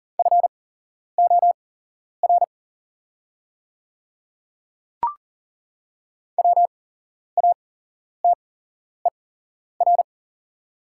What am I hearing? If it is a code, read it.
Morse code, heard as a keyed tone, about 40 wpm, spaced as for 11 wpm, 700 Hz. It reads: FOR WATER
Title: 8x Character Spacing - Top 200 Words - 40wpm